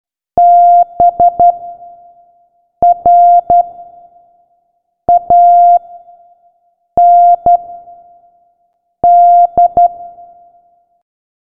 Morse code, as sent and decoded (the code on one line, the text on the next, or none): -... / .-. / .- / -. / -..
B R A N D